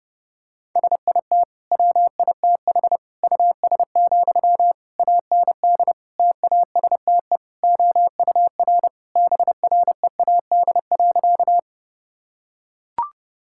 Morse code, heard as a keyed tone, about 30 wpm, 700 Hz. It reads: SIT WITH US, AND TASTE OUR BREAD.
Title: Sentences from Top 1600 Words 30wpm